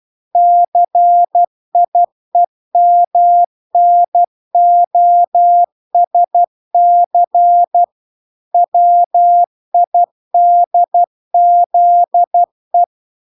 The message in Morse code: CIEMNOSC WIDZE

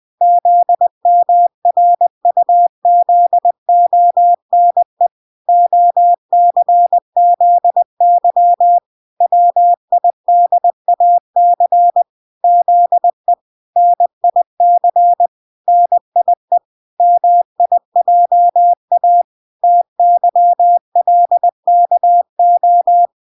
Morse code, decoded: ZMRUZONE OCZY WIDAC ZE NIC NIE MIJA TYLKO

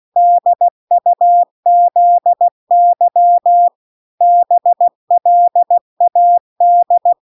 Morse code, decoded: DUZY BLAD

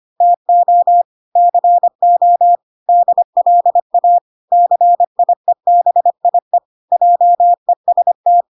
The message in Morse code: TO CO DLA CIEBIE JEST

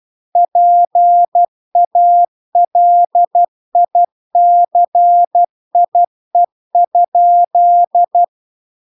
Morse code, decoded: PALICIE?

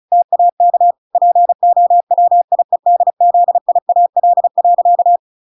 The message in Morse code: TAK POWIEDZIAL.

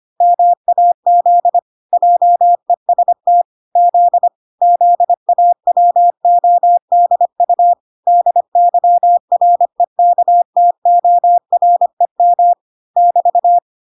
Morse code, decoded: MAZ JEST Z ZAWODU DYREKTOREM <BT>

